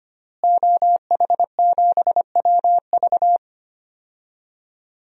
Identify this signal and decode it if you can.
Morse code, heard as a keyed tone, about 25 wpm, 700 Hz. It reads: OH7WV
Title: DXCC-100 Call Signs - 5 Characters - Repeated 3x - 25wpm